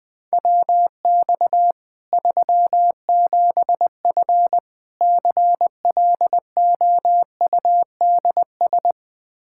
Morse code, heard as a keyed tone, about 20 wpm, 700 Hz. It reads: WX 37F CLOUDS
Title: Practice QSO - Exchange 2 - Info - 20wpm